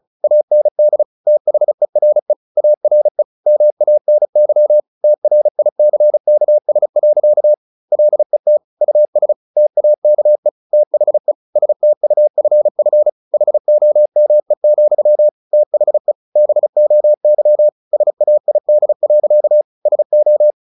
Morse code, heard as a keyed tone, about 34 wpm, 600 Hz. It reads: AND THERE ARE MANY TRICKS. LET US TAKE THE STUFF HOME, THE BOY SAID. SO
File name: the old man and the sea / the old man and the sea 1 35wpm